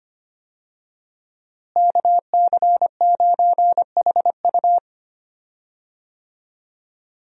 Morse code, heard as a keyed tone, about 25 wpm, 700 Hz. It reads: KC9HU
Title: USA Call Signs - 5 Characters - Repeated 3x - 25wpm